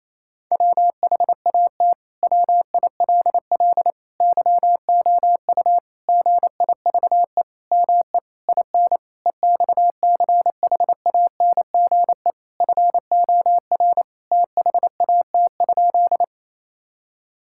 Morse code, heard as a keyed tone, about 28 wpm, 700 Hz. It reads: WHAT WILL YOU GIVE ME IN EXCHANGE FOR THAT?
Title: Sentences from Top 1000 Words - Review Part 6 - 28wpm